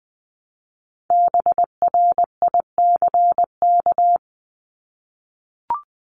Morse code, heard as a keyed tone, about 20 wpm, 700 Hz. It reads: BRICK
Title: Top 1751-2000 Words 20wpm